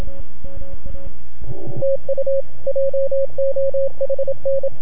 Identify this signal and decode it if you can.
Morse code, heard as a keyed tone, about 25 wpm, 530 Hz, with radio noise. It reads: TU JOHN
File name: NAQPCW-8-2009 / W1UJ NAQP CW 09 0100Z